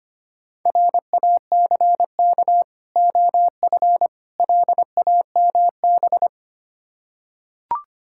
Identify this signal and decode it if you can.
Morse code, heard as a keyed tone, about 25 wpm, 690 Hz. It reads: RACK OF LAMB
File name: practice / 25wpm / Difficult Words with Hints - 25wpm